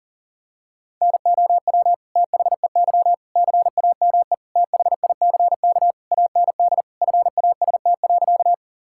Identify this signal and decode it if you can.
Morse code, heard as a keyed tone, about 40 wpm, 700 Hz. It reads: NOW THEY CAME THICK AND FAST.
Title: Sentences from Top 1600 Words 40wpm